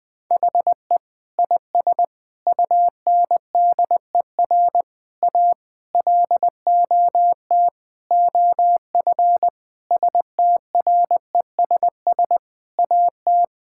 Morse code, heard as a keyed tone, about 20 wpm, 700 Hz. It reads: HE IS UNDER A LOT OF STRESS AT
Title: Sentences from Top 1000 Words - Review Part 6 - 20wpm